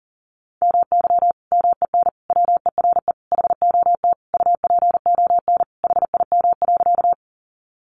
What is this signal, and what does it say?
Morse code, heard as a keyed tone, about 40 wpm, 700 Hz. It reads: MY MEN WERE HOT UPON HIM.
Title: Sentences from Top 1100 Words 40wpm